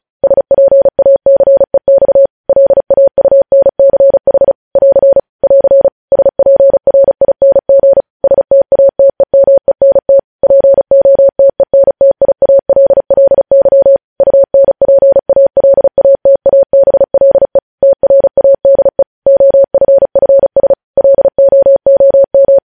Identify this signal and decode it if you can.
Morse code, heard as a keyed tone, about 36 wpm, 550 Hz. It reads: SPACEX LAUNCH <AR> <AR> SPRING STATEMENT POTENTIALLY UNPALATABLE TRADE OFFS LOOM
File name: podcast / News headlines-1x-35x35